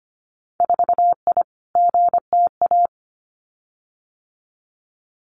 Morse code, heard as a keyed tone, about 25 wpm, 700 Hz. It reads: 4I GTA
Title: Winter Field Day Exchanges Encore 25wpm